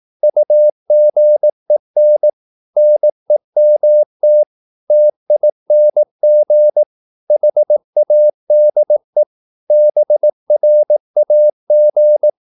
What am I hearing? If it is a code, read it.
Morse code, heard as a keyed tone, about 18 wpm, 600 Hz. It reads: UGEN NEMT TING HADE BRAG
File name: Korte ord DK 18wpm